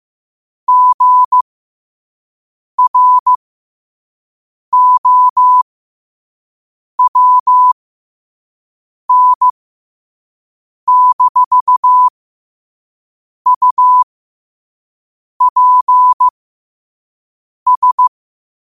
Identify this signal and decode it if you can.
Morse code, heard as a keyed tone, about 15 wpm, spaced as for 5 wpm, 1000 Hz. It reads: GROWN-UPS
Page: New quotes every day in morse code at 5 Words per minute.